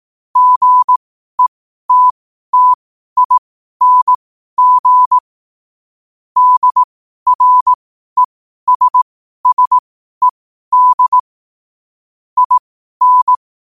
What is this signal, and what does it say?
Morse code, heard as a keyed tone, about 18 wpm, spaced as for 12 wpm, 1000 Hz. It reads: GETTING DRESSED IN
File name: qotdfiles / 12wpm